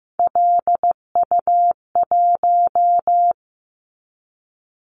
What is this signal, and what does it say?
Morse code, heard as a keyed tone, about 15 wpm, 700 Hz. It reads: LU1
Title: Callsign Prefixes for Top 100 DXCC - 15wpm